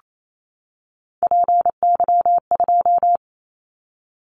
Morse code, heard as a keyed tone, about 28 wpm, 700 Hz. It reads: PY2
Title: Callsign Prefixes for Top 100 DXCC - 28wpm